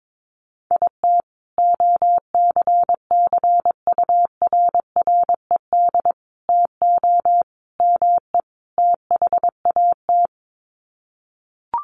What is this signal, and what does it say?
Morse code, heard as a keyed tone, about 22 wpm, 700 Hz. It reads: IT OCCURRED TO ME THAT
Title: Sets of 5 Words Encore 22wpm